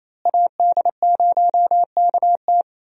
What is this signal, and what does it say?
Morse code, heard as a keyed tone, about 28 wpm, 700 Hz. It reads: AD0KT
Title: USA Call Signs - 5 Characters - Repeated 2x - 28wpm